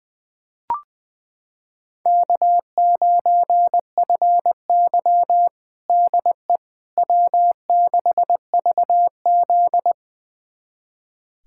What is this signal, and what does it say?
Morse code, heard as a keyed tone, about 21 wpm, 700 Hz. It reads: K9FY DE W6VZ
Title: USA Callsign DE Callsign - 4 Characters - 20wpm